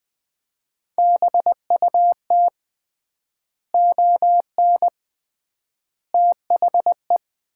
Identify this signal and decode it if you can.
Morse code, heard as a keyed tone, about 20 wpm, 700 Hz. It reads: BUT ON THE
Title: 3x Word-Spacing - Sets of 5 Words - 20wpm